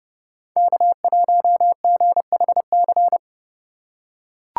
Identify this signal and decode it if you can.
Morse code, heard as a keyed tone, about 29 wpm, 700 Hz. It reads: K1GHC